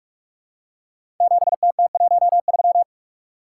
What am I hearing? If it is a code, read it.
Morse code, heard as a keyed tone, about 44 wpm, 700 Hz. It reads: 7TT13
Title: Code Groups - 5 Letter:Numbers - 45wpm